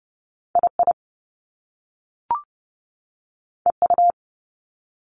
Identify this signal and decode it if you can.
Morse code, heard as a keyed tone, about 30 wpm, 700 Hz. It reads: I I E U
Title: Top Two Letter Combinations in Words 30wpm